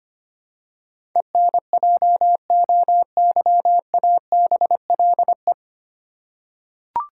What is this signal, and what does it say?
Morse code, heard as a keyed tone, about 25 wpm, 700 Hz. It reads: ENJOYABLE
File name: Nine Letter Words - 25wpm